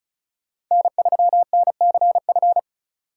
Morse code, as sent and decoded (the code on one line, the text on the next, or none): -. ...-- -. -.-. ..-.
N3NCF